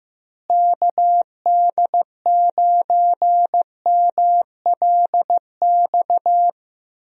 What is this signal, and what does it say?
Morse code, heard as a keyed tone, about 15 wpm, 700 Hz. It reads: KD9MLX
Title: USA Call Signs - 6 Characters - Repeated 3x - 15wpm